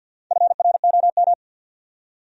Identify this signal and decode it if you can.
Morse code, heard as a keyed tone, about 52 wpm, 700 Hz. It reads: FROM
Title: Top 100 Words 50wpm